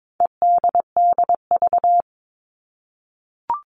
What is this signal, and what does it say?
Morse code, heard as a keyed tone, about 22 wpm, 700 Hz. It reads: EDDV
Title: All Cut Numbers - 4-Digit 22wpm